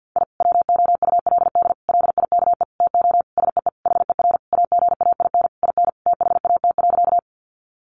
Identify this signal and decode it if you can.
Morse code, heard as a keyed tone, about 50 wpm, 700 Hz. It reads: I WOULD LIKE TO SEE HER AGAIN IN THAT.
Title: Sentences from Top 200 Words 50wpm